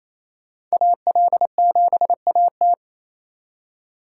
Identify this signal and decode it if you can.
Morse code, heard as a keyed tone, about 35 wpm, 700 Hz. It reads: AL7AT
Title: Speed Racing - USA 2x2 Call Signs 28wpm